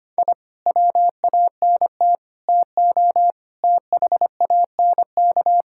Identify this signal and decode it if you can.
Morse code, heard as a keyed tone, about 25 wpm, 700 Hz. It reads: I WANT TO THANK
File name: Sets of 9 Words - 25wpm